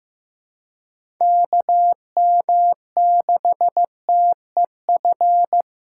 Morse code, heard as a keyed tone, about 15 wpm, 700 Hz. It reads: KM6TEF
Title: USA Call Signs - 6 Characters - 2x Character Spacing - 15wpm